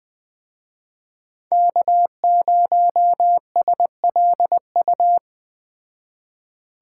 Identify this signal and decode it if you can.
Morse code, heard as a keyed tone, about 20 wpm, 700 Hz. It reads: K0SLU